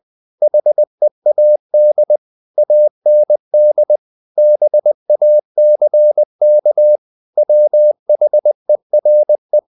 Morse code, heard as a keyed tone, about 20 wpm, 600 Hz. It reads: HEAD AND BACK WHERE
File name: the old man and the sea part 2 20wpm